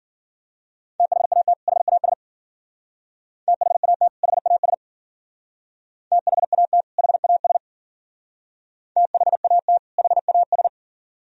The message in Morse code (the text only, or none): THAT HAS THAT HAS THAT HAS THAT HAS